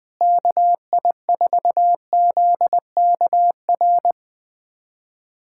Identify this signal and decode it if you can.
Morse code, heard as a keyed tone, about 20 wpm, 700 Hz. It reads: KI4ZKR
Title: USA Callsign DE Callsign - 6 Characters - 20wpm